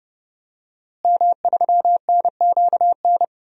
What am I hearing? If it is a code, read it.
Morse code, heard as a keyed tone, about 30 wpm, 700 Hz. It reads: M3NQN
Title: DXCC-100 Call Signs - 5 Characters - Repeated 2x - 30wpm